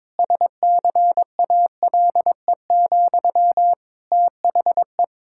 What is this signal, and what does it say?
Morse code, heard as a keyed tone, about 22 wpm, 700 Hz. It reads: SCALE, THE
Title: Sentences from Top 1000 Words - Review Part 5 - 22wpm